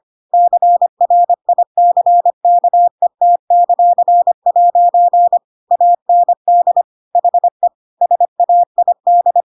Morse code, heard as a keyed tone, about 25 wpm, 700 Hz. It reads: CRICKET;' AND HE SAID
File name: Pinocchio / Pinocchio ch7 25 25 ik1hge